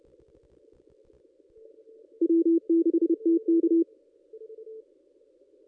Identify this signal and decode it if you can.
Morse code, heard as a keyed tone, about 30 wpm, 330 Hz. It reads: W6TK
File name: contests / ss-cw-k5tr-op-n5rz -1800